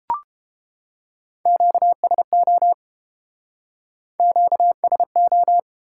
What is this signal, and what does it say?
Morse code, heard as a keyed tone, about 29 wpm, 700 Hz. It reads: QSO QSO